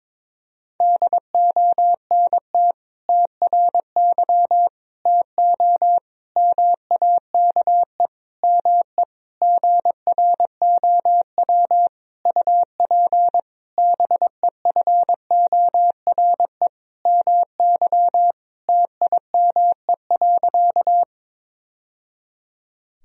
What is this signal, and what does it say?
Morse code, heard as a keyed tone, about 21 wpm, 700 Hz. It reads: DONT TRY TO MAKE ME GROW UP BEFORE MY TIME.
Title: Sentences from Top 1000 Words - Review Part 1 - 22wpm